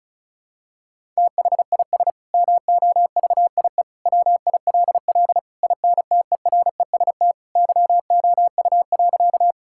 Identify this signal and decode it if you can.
Morse code, heard as a keyed tone, about 36 wpm, 700 Hz. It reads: THIS MOVIE WILL INTEREST YOU.
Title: Sentences from Top 1000 Words - Review Part 5 - 35wpm